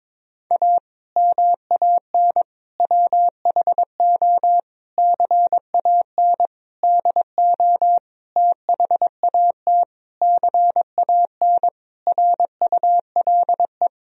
Morse code, heard as a keyed tone, about 22 wpm, 700 Hz. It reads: A MAN WHO CAN DO THAT CAN RULE
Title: Sentences from Top 1000 Words - Review Part 2 - 22wpm